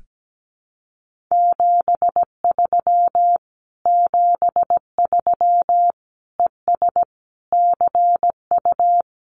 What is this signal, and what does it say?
Morse code, heard as a keyed tone, about 17 wpm, 700 Hz. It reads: 73 73 ES CU
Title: Common QSO Phrases Encore 17wpm